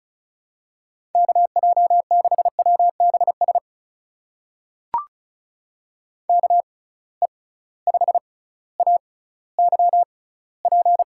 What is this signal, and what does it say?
Morse code, heard as a keyed tone, about 36 wpm, spaced as for 18 wpm, 700 Hz. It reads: KJ6WBS KE5AYP
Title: USA Call Signs - 6 Characters - 6x Character Spacing - 35wpm